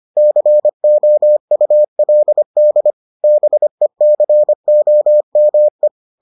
Morse code, heard as a keyed tone, about 25 wpm, 600 Hz. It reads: COULD BECOME